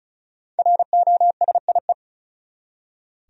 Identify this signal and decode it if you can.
Morse code, heard as a keyed tone, about 34 wpm, 700 Hz. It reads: ROSIE
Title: USA Common Names - Max 5 Letters - 35wpm